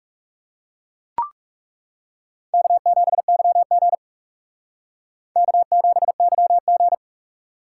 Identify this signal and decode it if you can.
Morse code, heard as a keyed tone, about 37 wpm, 700 Hz. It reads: K7YG K7YG
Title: Speed Racing - USA Call Signs 30wpm